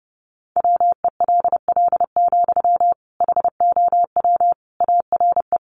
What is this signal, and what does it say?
Morse code, heard as a keyed tone, about 30 wpm, 700 Hz. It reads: WELL, HOW ARE
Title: Sentences from Top 200 Words 30wpm